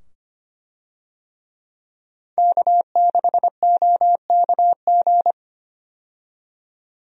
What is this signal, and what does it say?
Morse code, heard as a keyed tone, about 25 wpm, 700 Hz. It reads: K6OKG
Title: USA Call Signs - 5 Characters - 25wpm